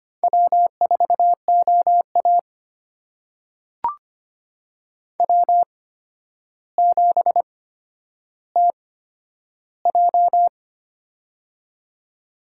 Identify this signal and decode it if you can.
Morse code, heard as a keyed tone, about 25 wpm, spaced as for 11 wpm, 700 Hz. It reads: W4OA W7TJ
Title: USA Call Signs - 4 Characters - 8x Character Spacing - 25wpm